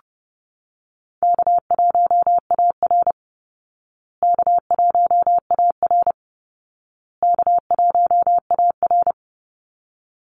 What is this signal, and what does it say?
Morse code, heard as a keyed tone, about 30 wpm, 700 Hz. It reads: K1AR K1AR K1AR
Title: Easy Call Signs Repeated 3x - 30wpm